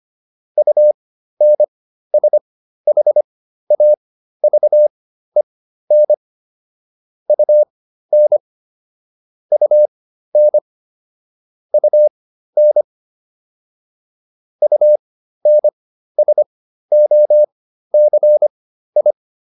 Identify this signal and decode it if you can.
Morse code, heard as a keyed tone, about 25 wpm, spaced as for 13 wpm, 600 Hz. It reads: UNSHAVEN UN UN UN UNSOCI